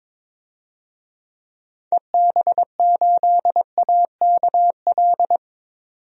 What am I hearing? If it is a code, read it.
Morse code, heard as a keyed tone, about 21 wpm, 700 Hz. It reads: EB8AKL
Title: DXCC-100 Call Signs - 6 Characters - 22wpm